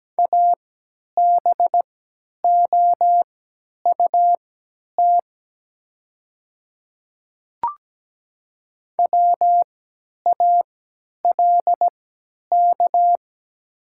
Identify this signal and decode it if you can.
Morse code, heard as a keyed tone, about 17 wpm, spaced as for 10 wpm, 700 Hz. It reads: ABOUT WALK
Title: 3x Character Spacing - Top 300 Words - 17wpm